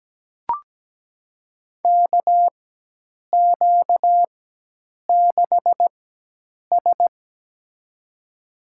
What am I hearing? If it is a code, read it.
Morse code, heard as a keyed tone, about 17 wpm, spaced as for 8 wpm, 700 Hz. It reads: KQ6S
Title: USA Call Signs - 4 Characters - 4x Character Spacing - 17wpm